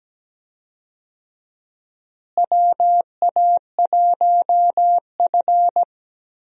Morse code, heard as a keyed tone, about 17 wpm, 700 Hz. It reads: WA1F
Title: USA Call Signs - 4 Characters - Repeated 2x - 17wpm